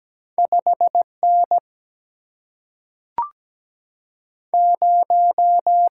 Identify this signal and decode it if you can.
Morse code, heard as a keyed tone, about 17 wpm, 700 Hz. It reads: 5N 0
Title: Single-Letter Single-Number 17wpm